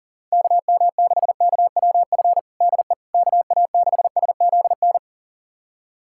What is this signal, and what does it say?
Morse code, heard as a keyed tone, about 37 wpm, 700 Hz. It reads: KM6KWF DE KA6SZN
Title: USA Callsign DE Callsign - 6 Characters - 40wpm